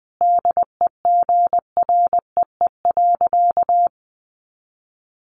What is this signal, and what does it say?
Morse code, heard as a keyed tone, about 19 wpm, 700 Hz. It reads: DEGREE.
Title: Sentences from Top 1500 Words 20wpm